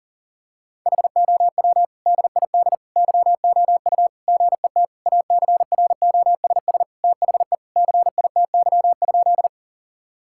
Morse code, heard as a keyed tone, about 37 wpm, 700 Hz. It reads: HOW DID YOU GET ACROSS THE CITY?
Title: Sentences from Top 1000 Words - Review Part 5 - 40wpm